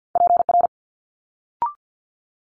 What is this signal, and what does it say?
Morse code, heard as a keyed tone, about 50 wpm, 700 Hz. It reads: PR
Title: Multi-letter Phonemes Quick Fire 50wpm